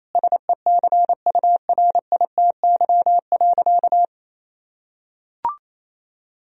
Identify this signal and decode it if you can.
Morse code, heard as a keyed tone, about 27 wpm, 700 Hz. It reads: SECURITY.
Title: Sentences from Top 1000 Words - Review Part 4 - 28wpm